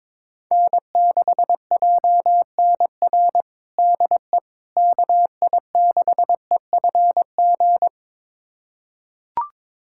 Morse code, heard as a keyed tone, about 22 wpm, 700 Hz. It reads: N6JNR DE KI6EFG
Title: USA Callsign DE Callsign 22wpm